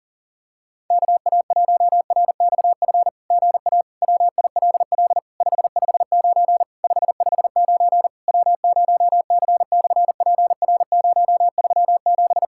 Morse code, heard as a keyed tone, about 41 wpm, 700 Hz. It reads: KA1RXF GA WILL 559 559 W0C/PR037
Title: SOTA QSO Practice 40wpm